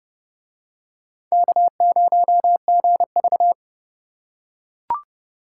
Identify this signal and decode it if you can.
Morse code, heard as a keyed tone, about 30 wpm, 700 Hz. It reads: K0GV